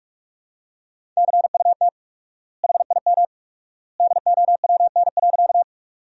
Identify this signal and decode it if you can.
Morse code, heard as a keyed tone, about 46 wpm, 700 Hz. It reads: CUT HIM DOWN.
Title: 4x Word-Spacing - Sentences from Top 300 Words - 45wpm